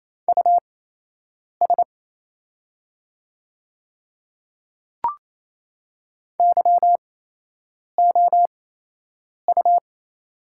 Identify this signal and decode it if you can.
Morse code, heard as a keyed tone, about 27 wpm, spaced as for 8 wpm, 690 Hz. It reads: US YOU
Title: 8x Character Spacing - Top 200 Words - 28wpm